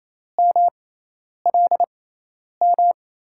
Morse code, heard as a keyed tone, about 28 wpm, spaced as for 10 wpm, 700 Hz. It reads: MLM
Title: USA Call Signs - 5 Characters - 6x Character Spacing - 28wpm